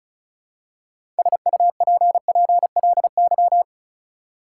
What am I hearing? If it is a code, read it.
Morse code, heard as a keyed tone, about 36 wpm, 700 Hz. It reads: SUPPLY
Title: Six Letter Words - 35wpm